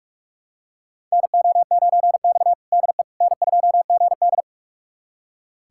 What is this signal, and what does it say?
Morse code, heard as a keyed tone, about 42 wpm, 700 Hz. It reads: NO9X DE N2GD
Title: USA Callsign DE Callsign - 4 Characters - 45wpm